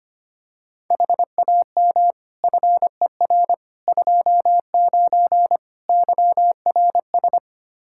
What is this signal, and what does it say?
Morse code, heard as a keyed tone, about 25 wpm, 700 Hz. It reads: HAM FER 29 YRS
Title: K1USN SST QSO Practice 25wpm